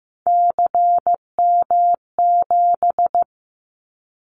Morse code, heard as a keyed tone, about 15 wpm, 700 Hz. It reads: CM7
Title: Callsign Prefixes for Top 100 DXCC - 15wpm